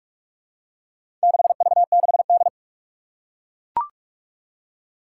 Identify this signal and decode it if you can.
Morse code, heard as a keyed tone, about 41 wpm, 700 Hz. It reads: 6V6D
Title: Four Challenging Letter-Number Combinations 45wpm